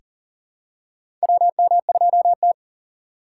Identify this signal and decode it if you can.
Morse code, heard as a keyed tone, about 40 wpm, 700 Hz. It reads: WM2T
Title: USA Call Signs - 4x Character Spacing - 40wpm